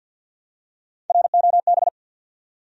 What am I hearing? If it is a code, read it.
Morse code, heard as a keyed tone, about 48 wpm, 700 Hz. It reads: ROB
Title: USA Common Names - Max 5 Letters - 50wpm